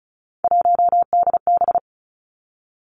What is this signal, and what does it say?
Morse code, heard as a keyed tone, about 36 wpm, 700 Hz. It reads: 1DB
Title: Three Challenging Letter-Number Combinations 35wpm